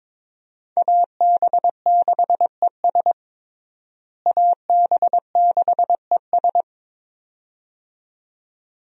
Morse code, heard as a keyed tone, about 22 wpm, 700 Hz. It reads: AB6ES AB6ES
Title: USA Call Signs - 5 Characters - Repeated 2x - 22wpm